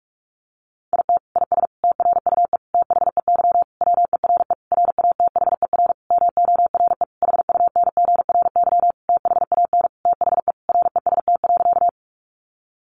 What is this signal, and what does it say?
Morse code, heard as a keyed tone, about 44 wpm, 700 Hz. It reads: IT IS TRUE THEY WERE RATHER MORE HUNGRY THAN THE REST.
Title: Sentences from Top 1400 Words 45wpm